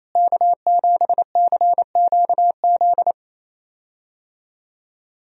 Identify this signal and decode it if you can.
Morse code, heard as a keyed tone, about 28 wpm, 700 Hz. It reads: K7CQZ
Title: USA Call Signs - 5 Characters - Repeated 3x - 28wpm